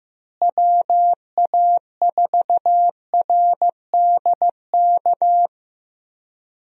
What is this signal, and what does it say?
Morse code, heard as a keyed tone, about 15 wpm, 700 Hz. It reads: WA4RDK
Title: USA Call Signs - 6 Characters - Repeated 3x - 15wpm